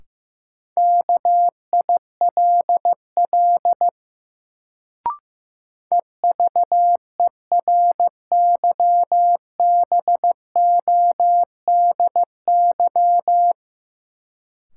Morse code, heard as a keyed tone, about 15 wpm, 700 Hz. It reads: KILL EVERYBODY
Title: Sentences from Top 1400 Words - New Word Drill Alt - 15wpm